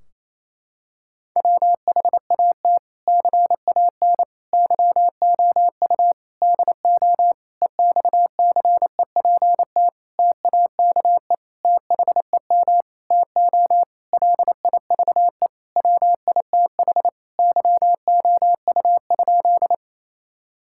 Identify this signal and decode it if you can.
Morse code, heard as a keyed tone, about 28 wpm, 700 Hz. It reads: WHAT CAN YOU DO EXCEPT TAKE THEM TO LIVE WITH YOU?
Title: Sentences from Top 1600 Words 28wpm